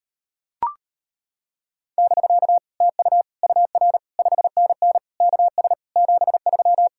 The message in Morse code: <BK> TU UR 5NN KS 73

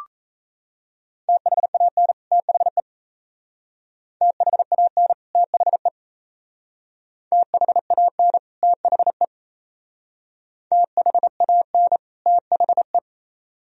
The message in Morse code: THAN THE THAN THE THAN THE THAN THE

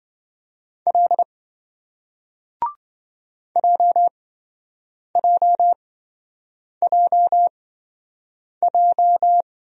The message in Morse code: L JJJJ

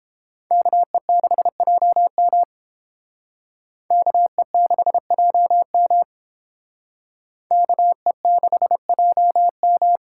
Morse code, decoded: KE6JM KE6JM KE6JM

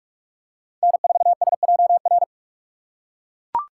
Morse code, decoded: N4SJR

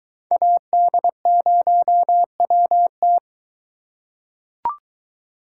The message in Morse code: AD0WT

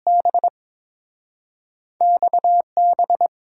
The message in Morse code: B XB

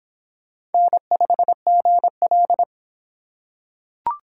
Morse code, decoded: N5GL

